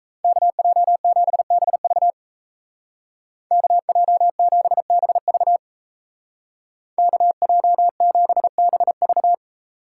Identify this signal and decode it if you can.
Morse code, heard as a keyed tone, about 35 wpm, 700 Hz. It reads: KJ7BV KJ7BV KJ7BV